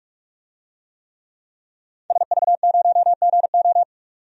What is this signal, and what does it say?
Morse code, heard as a keyed tone, about 46 wpm, 700 Hz. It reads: SV0GO
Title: DXCC-100 Call Signs - 5 Characters - 45wpm